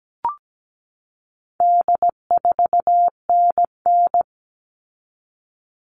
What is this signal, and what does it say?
Morse code, heard as a keyed tone, about 17 wpm, 700 Hz. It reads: D4NN
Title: All Cut Numbers - 4-Digit 17wpm